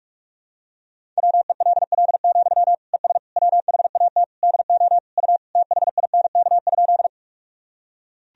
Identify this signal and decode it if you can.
Morse code, heard as a keyed tone, about 46 wpm, 700 Hz. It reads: WELL, ES WHAT DO U THINK?
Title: Sentences from Top 200 Words and Abbreviations 45wpm